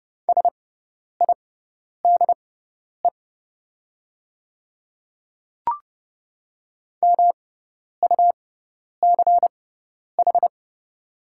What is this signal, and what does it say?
Morse code, heard as a keyed tone, about 29 wpm, spaced as for 10 wpm, 700 Hz. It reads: SIDE MUCH
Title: 6x Character Spacing - Top 300 Words - 30wpm